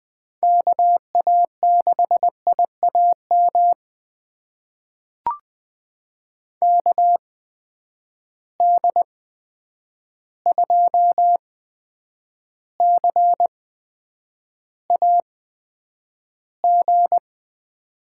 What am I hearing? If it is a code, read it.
Morse code, heard as a keyed tone, about 20 wpm, spaced as for 9 wpm, 700 Hz. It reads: KA6IAM KD2CAG
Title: USA Call Signs - 6 Characters - 8x Character Spacing - 20wpm